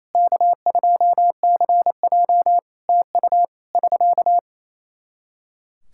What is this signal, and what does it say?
Morse code, heard as a keyed tone, about 27 wpm, 700 Hz. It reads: K2CJ TU <SK>
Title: K1USN SST QSO Practice 28wpm